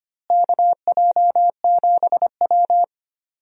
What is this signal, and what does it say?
Morse code, heard as a keyed tone, about 25 wpm, 700 Hz. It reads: KJ7W
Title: USA Call Signs - 4 Characters - Repeated 2x - 25wpm